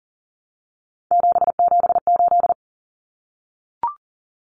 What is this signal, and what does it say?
Morse code, heard as a keyed tone, about 40 wpm, 700 Hz. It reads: 778
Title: Common Cut Numbers - 3-Digit 40wpm